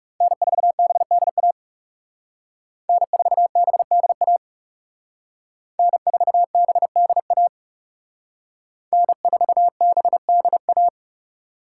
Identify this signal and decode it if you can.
Morse code, heard as a keyed tone, about 36 wpm, 700 Hz. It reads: N4BDA N4BDA N4BDA N4BDA
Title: Speed Racing - USA 1x3 Call Signs 30wpm